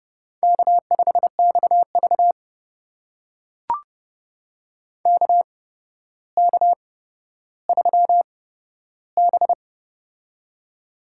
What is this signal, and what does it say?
Morse code, heard as a keyed tone, about 30 wpm, spaced as for 13 wpm, 700 Hz. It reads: K5XV KK3B
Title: USA Call Signs - 4 Characters - 8x Character Spacing - 30wpm